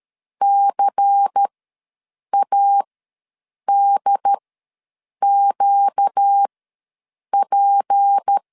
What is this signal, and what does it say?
Morse code, heard as a keyed tone, about 13 wpm, spaced as for 7 wpm, 800 Hz. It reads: CADQP
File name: LEZIONI / 6 PKX0 / 6 P ws 12wpm 3117